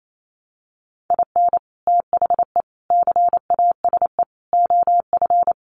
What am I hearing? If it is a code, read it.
Morse code, heard as a keyed tone, about 28 wpm, 700 Hz. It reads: IN THE CASE OF
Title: Sets of 4 Words 28wpm